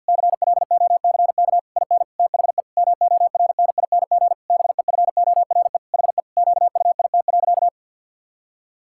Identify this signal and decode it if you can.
Morse code, heard as a keyed tone, about 50 wpm, 700 Hz. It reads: CLOCK IN THE MORNING BEFORE HE QUIT.
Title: Pangram Challenge v2 - 50wpm